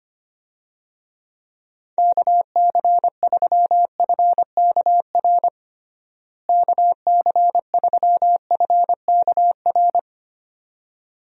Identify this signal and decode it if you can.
Morse code, heard as a keyed tone, about 25 wpm, 700 Hz. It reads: KC3FKR KC3FKR
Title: USA Call Signs - 6 Characters - Repeated 2x - 25wpm